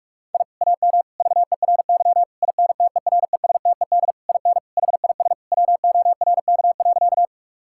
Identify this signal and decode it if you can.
Morse code, heard as a keyed tone, about 43 wpm, 700 Hz. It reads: I AM VERY INTERESTED IN HIS WORK.
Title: Sentences from Top 1000 Words - Review Part 6 - 45wpm